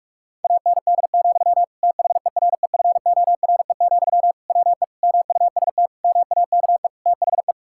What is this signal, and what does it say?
Morse code, heard as a keyed tone, about 46 wpm, 690 Hz. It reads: AND, THEREFORE, WE MUST MAKE THE
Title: Sentences from Top 1000 Words - Review Part 5 - 45wpm